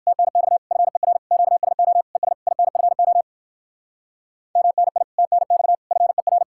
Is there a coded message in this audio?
Yes: Morse code from 0.1 to 6.4 s, heard as a keyed tone, about 60 wpm, 700 Hz. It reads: TNX FER QSO ES INFO MNI TNX FER